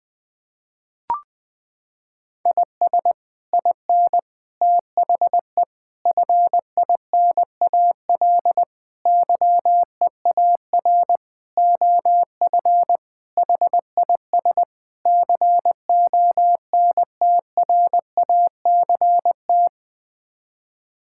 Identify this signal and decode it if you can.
Morse code, heard as a keyed tone, about 20 wpm, 700 Hz. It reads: IS IN THE FINAL YEAR OF HIS CONTRACT
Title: Sets of 8 Words - 20wpm